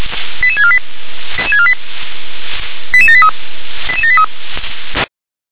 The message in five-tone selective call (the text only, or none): 88648 88783 88783